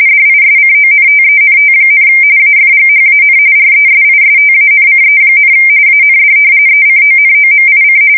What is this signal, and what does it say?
RTTY, about 45.45 baud. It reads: DISCONTENT, AFGHAN RESISTANCE, AND ECONOMIC
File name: fsk170 / battleshipnj